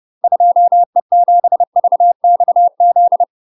Morse code, heard as a keyed tone, about 30 wpm, 700 Hz. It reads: 2E7VXZ
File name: UK MASTER QSOs TRAINING / 30WPM / UK MASTER QSOs 0044